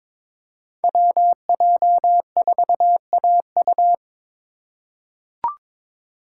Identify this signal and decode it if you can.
Morse code, heard as a keyed tone, about 22 wpm, 700 Hz. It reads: WJ4AU
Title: Code Groups - 5 Letter:Numbers - 22wpm